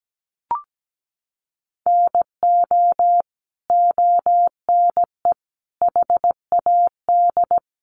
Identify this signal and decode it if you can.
Morse code, heard as a keyed tone, about 17 wpm, 700 Hz. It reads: NO ONE HAD
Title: Sentences from Top 1400 Words 17wpm